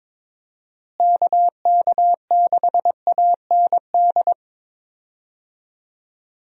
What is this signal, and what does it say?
Morse code, heard as a keyed tone, about 22 wpm, 700 Hz. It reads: KK6AND
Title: USA Call Signs - 6 Characters - Repeated 2x - 22wpm